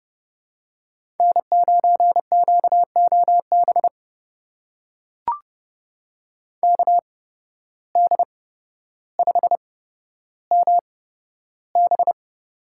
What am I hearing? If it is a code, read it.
Morse code, heard as a keyed tone, about 30 wpm, spaced as for 13 wpm, 700 Hz. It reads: N9QOB KD5MB
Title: USA Call Signs - 5 Characters - 8x Character Spacing - 30wpm